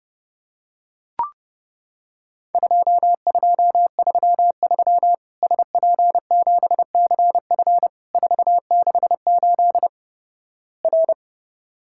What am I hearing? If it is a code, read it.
Morse code, heard as a keyed tone, about 32 wpm, 700 Hz. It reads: 2233 SP7CF 468 R
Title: Worked All Europe DX Contest - QTC Exchange - 30wpm